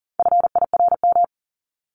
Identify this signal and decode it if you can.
Morse code, heard as a keyed tone, about 40 wpm, 700 Hz. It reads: FIRM
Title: Top 1251-1500 Words 40wpm